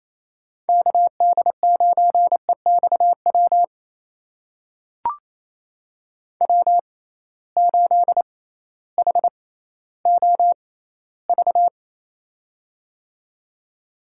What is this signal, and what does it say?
Morse code, heard as a keyed tone, about 28 wpm, spaced as for 15 wpm, 700 Hz. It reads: KD9EXW W8HOV
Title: USA Call Signs - 6x Character Spacing - 28wpm